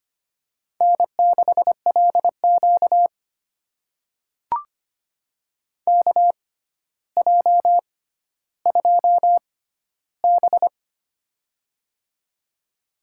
Morse code, heard as a keyed tone, about 25 wpm, spaced as for 13 wpm, 700 Hz. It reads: N6LQ KJ2B
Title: USA Call Signs - 4 Characters - 6x Character Spacing - 25wpm